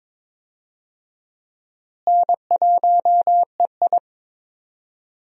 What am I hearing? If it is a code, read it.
Morse code, heard as a keyed tone, about 22 wpm, 700 Hz. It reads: N1EI
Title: USA Call Signs - 4 Characters - Repeated 2x - 22wpm